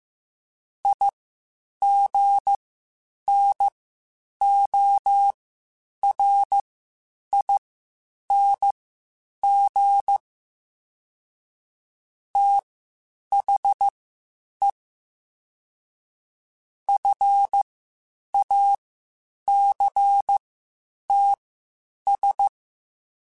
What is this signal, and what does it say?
Morse code, heard as a keyed tone, about 15 wpm, spaced as for 8 wpm, 800 Hz. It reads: IGNORING THE FACTS
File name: Quotes Spaced / Quotes Spaced-mp3s / IgnoringFacts sp @13